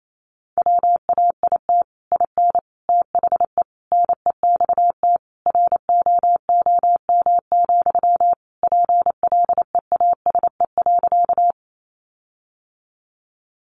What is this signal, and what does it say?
Morse code, heard as a keyed tone, about 28 wpm, 700 Hz. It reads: WAIT IN THE NEXT ROOM, PLEASE.
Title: Sentences from Top 1500 Words 28wpm